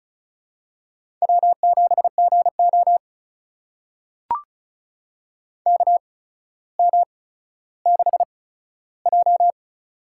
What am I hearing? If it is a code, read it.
Morse code, heard as a keyed tone, about 36 wpm, spaced as for 15 wpm, 700 Hz. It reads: W7GO KM6J
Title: USA Call Signs - 8x Character Spacing - 35wpm